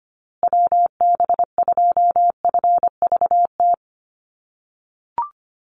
Morse code, heard as a keyed tone, about 25 wpm, 700 Hz. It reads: WB2FVT
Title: USA Call Signs Repeated 3x 25wpm